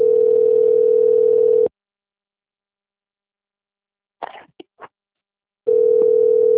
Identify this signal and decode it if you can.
Ringback tone (North America), 440 + 480 Hz.